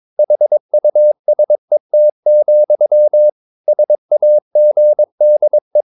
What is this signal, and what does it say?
Morse code, heard as a keyed tone, about 22 wpm, 600 Hz. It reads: HUSET, SAGDE
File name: Ferie QRS 22wpm